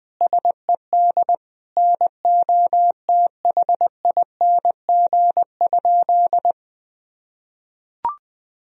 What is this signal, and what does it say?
Morse code, heard as a keyed tone, about 20 wpm, 700 Hz. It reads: SED NOTHING?
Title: Sentences from Top 500 Words and Abbreviations 20wpm